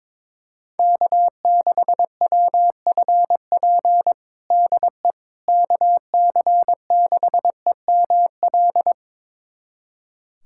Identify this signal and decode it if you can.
Morse code, heard as a keyed tone, about 22 wpm, 700 Hz. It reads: K6WFP DE KC6EML
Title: USA Callsign DE Callsign 22wpm